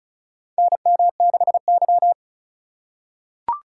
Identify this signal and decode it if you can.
Morse code, heard as a keyed tone, about 36 wpm, 700 Hz. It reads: NM6Y
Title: USA Call Signs - 4 Characters - 3x Character Spacing - 35wpm